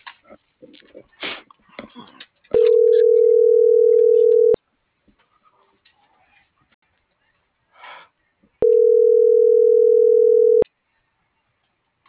Ringback tone (North America), 440 + 480 Hz.